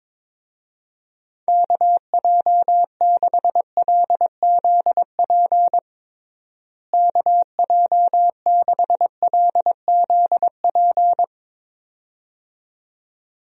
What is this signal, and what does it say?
Morse code, heard as a keyed tone, about 22 wpm, 700 Hz. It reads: KJ6LZP KJ6LZP
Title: USA Call Signs - 6 Characters - Repeated 2x - 22wpm